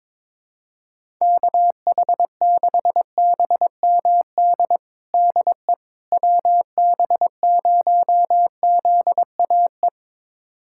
Morse code, heard as a keyed tone, about 22 wpm, 700 Hz. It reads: KH6BMD DE WB0ZAE
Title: USA Callsign DE Callsign - 6 Characters - 22wpm